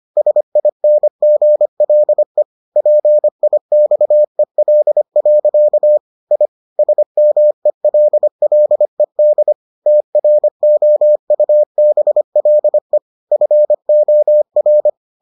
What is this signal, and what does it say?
Morse code, heard as a keyed tone, about 25 wpm, 600 Hz. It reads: SINGLE PIXEL. I SMELLED TROUBLE FOR